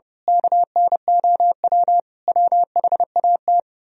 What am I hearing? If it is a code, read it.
Morse code, heard as a keyed tone, about 29 wpm, 700 Hz. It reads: KNOW WHAT
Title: Speed-Racing - Sets of 5 Words 30wpm